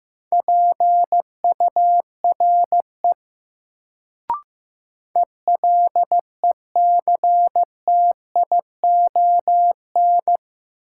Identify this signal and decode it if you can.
Morse code, heard as a keyed tone, about 15 wpm, 700 Hz. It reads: PURE ELECTION
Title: Sentences from Top 1500 Words - New Word Drill Alt - 15wpm